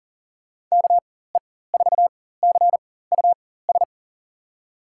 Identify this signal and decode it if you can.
Morse code, heard as a keyed tone, about 40 wpm, spaced as for 19 wpm, 700 Hz. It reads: KE4CUS
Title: USA Call Signs - 6 Characters - 4x Character Spacing - 40wpm